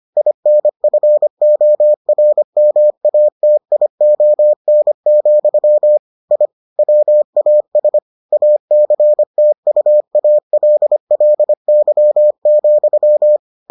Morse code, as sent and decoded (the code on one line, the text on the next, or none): .. -. ..-. --- .-. -- .- - .. --- -. --..-- / .. / .-- .- ... / .- -.-. - ..- .- .-.. .-.. -.-- --..--
INFORMATION, I WAS ACTUALLY,